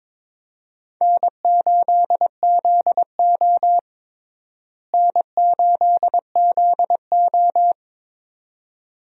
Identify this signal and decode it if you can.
Morse code, heard as a keyed tone, about 22 wpm, 700 Hz. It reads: N8ZO N8ZO
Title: USA Call Signs - 4 Characters - Repeated 2x - 22wpm